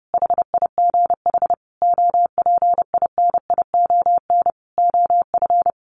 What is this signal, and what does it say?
Morse code, heard as a keyed tone, about 30 wpm, 700 Hz. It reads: HIGH OPINION OF
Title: Sentences from Top 1300 Words 30wpm